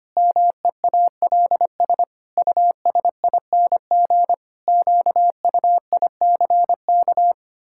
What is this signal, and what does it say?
Morse code, heard as a keyed tone, about 25 wpm, 700 Hz. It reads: MEALS USING QUICK
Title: Pangram Challenge v2 - 25wpm